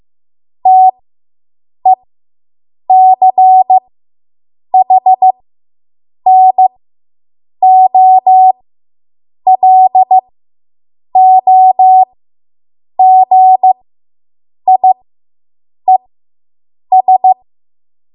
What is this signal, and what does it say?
Morse code, heard as a keyed tone, about 15 wpm, spaced as for 7 wpm, 750 Hz. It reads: TECHNOLOGIES